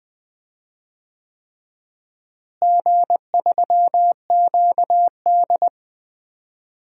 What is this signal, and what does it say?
Morse code, heard as a keyed tone, about 20 wpm, 700 Hz. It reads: G3QD
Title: DXCC-100 Call Signs - 4 Characters - Repeated 3x - 20wpm